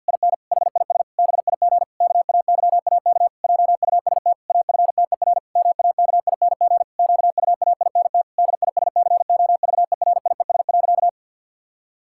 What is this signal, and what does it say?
Morse code, heard as a keyed tone, about 50 wpm, 700 Hz. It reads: IN HIS BIG KAYAK JUST AFTER MAKING QUAINT DISCOVERIES.
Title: Pangram Challenge v2 - 50wpm